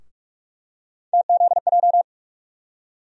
Morse code, heard as a keyed tone, about 45 wpm, 700 Hz. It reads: TZJ
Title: USA Call Sign Postfixes - 3 Letters 45wpm